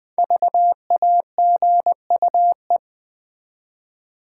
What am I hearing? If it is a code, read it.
Morse code, heard as a keyed tone, about 20 wpm, 700 Hz. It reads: VAGUE
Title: 5 Letter Words - First 3 Letters Given - 20wpm